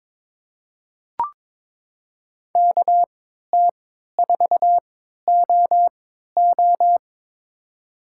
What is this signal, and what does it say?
Morse code, heard as a keyed tone, about 23 wpm, spaced as for 13 wpm, 700 Hz. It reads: KT4OO
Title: USA Call Signs - 3x Character Spacing - 22wpm